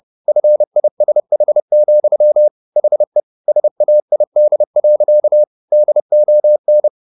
Morse code, heard as a keyed tone, about 30 wpm, 600 Hz. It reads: FISH, HE SAID. DON